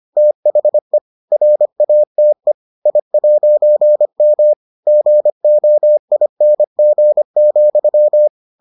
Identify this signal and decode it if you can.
Morse code, heard as a keyed tone, about 25 wpm, 600 Hz. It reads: THE RATE I'M GOING,